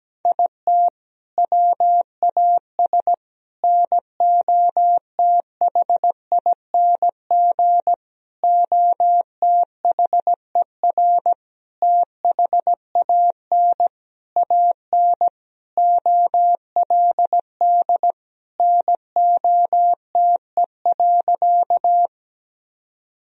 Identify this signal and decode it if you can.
Morse code, heard as a keyed tone, about 17 wpm, 700 Hz. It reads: IT WAS NOTHING OTHER THAN AN OLD NOTE.
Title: Sentences from Top 1000 Words - Review Part 2 - 17wpm